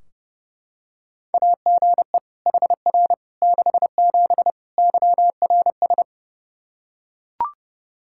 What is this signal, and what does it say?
Morse code, heard as a keyed tone, about 30 wpm, 700 Hz. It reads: AGE HR 67 YRS
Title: K1USN SST QSO Practice 30wpm